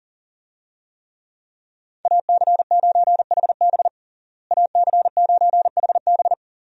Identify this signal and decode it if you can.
Morse code, heard as a keyed tone, about 40 wpm, 700 Hz. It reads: AC9HB AC9HB
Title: USA Call Signs - 5 Characters - Repeated 3x - 40wpm